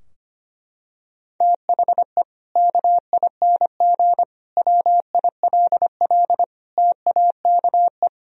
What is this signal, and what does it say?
Morse code, heard as a keyed tone, about 25 wpm, 700 Hz. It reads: THE KING WILL TAKE
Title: Sentences from Top 1000 Words - Review Part 4 - 25wpm